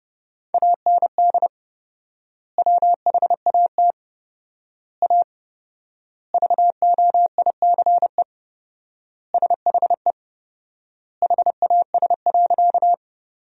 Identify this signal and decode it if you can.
Morse code, heard as a keyed tone, about 30 wpm, 700 Hz. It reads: AND WHAT A VOICE SHE HAS.
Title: 4x Word-Spacing - Sentences from Top 500 Words - 30wpm